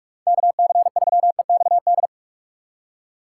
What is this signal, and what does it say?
Morse code, heard as a keyed tone, about 45 wpm, 700 Hz. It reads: KC3EXD
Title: USA Call Signs - 6 Characters - 2x Character Spacing - 45wpm